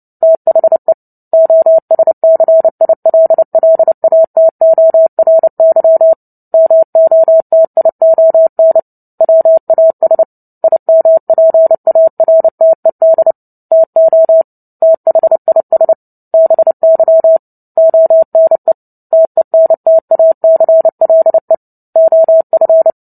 Morse code, heard as a keyed tone, about 30 wpm, 650 Hz. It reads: THE OSCILLATORY MOTION WAS IMPARTED TO THIS BY ONE TENTACLE OF
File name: Book Two / 03-Chapter 3 - 29 WPM